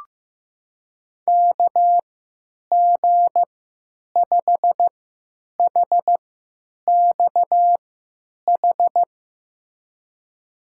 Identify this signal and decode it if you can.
Morse code, heard as a keyed tone, about 15 wpm, spaced as for 9 wpm, 700 Hz. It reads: KG5HXH
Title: USA Call Signs - 6 Characters - 3x Character Spacing - 15wpm